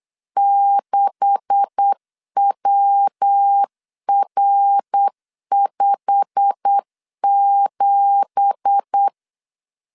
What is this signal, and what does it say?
Morse code, heard as a keyed tone, about 8 wpm, 800 Hz. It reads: 6WR57